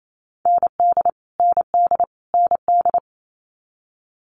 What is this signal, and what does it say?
Morse code, heard as a keyed tone, about 28 wpm, 700 Hz. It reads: ND ND ND
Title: Multi-letter Phonemes Repeated 3x 28wpm